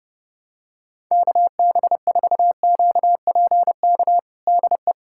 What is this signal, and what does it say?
Morse code, heard as a keyed tone, about 30 wpm, 700 Hz. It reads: KB4QPK DE
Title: USA Callsign DE Callsign 30wpm